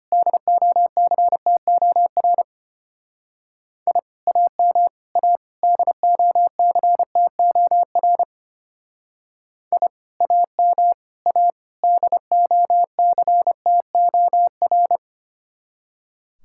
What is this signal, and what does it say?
Morse code, heard as a keyed tone, about 30 wpm, 700 Hz. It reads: DOCTOR I AM A DOCTOR I AM A DOCTOR
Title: Speed-Racing Common QSO Phrases - 25wpm